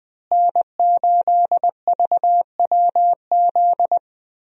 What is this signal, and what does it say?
Morse code, heard as a keyed tone, about 20 wpm, 700 Hz. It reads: N8VWZ
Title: USA Call Signs - 5 Characters - Repeated 3x - 20wpm